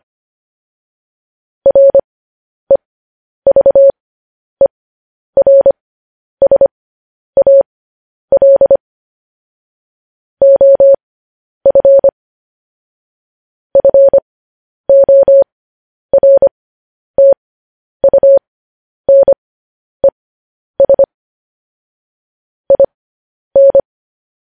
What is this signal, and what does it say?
Morse code, heard as a keyed tone, about 25 wpm, spaced as for 10 wpm, 550 Hz. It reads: REVERSAL OF FORTUNES IN
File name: podcast / News headlines-1x-25x10